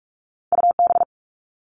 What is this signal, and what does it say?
Morse code, heard as a keyed tone, about 46 wpm, 700 Hz. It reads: UB
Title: All Cut Numbers - 2-Digit 45wpm